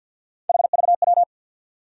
Morse code, heard as a keyed tone, about 54 wpm, 700 Hz. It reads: HVW
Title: USA Call Sign Postfixes - 3 Letters 50wpm